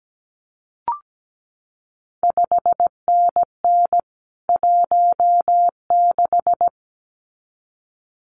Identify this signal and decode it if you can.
Morse code, heard as a keyed tone, about 17 wpm, 700 Hz. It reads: 5NN 16
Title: CQ WW DX Contest Exchange - 17wpm